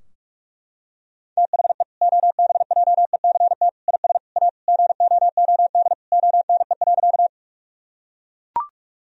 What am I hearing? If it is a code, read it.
Morse code, heard as a keyed tone, about 41 wpm, 700 Hz. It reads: THE OBJECT IS A GOOD ONE.
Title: Sentences from Top 1000 Words - Review Part 3 - 45wpm